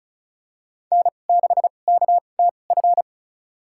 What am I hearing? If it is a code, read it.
Morse code, heard as a keyed tone, about 36 wpm, spaced as for 25 wpm, 700 Hz. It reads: N6KTF
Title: USA Call Signs - 5 Characters - 2x Character Spacing - 35wpm